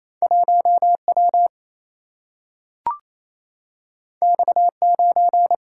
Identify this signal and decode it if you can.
Morse code, heard as a keyed tone, about 28 wpm, 700 Hz. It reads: 1W X9
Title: Single-Letter Single-Number 28wpm